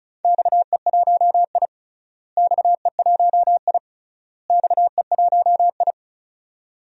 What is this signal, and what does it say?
Morse code, heard as a keyed tone, about 34 wpm, 700 Hz. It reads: XE1I XE1I XE1I
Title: DXCC-100 Call Signs - 4 Characters - Repeated 3x - 35wpm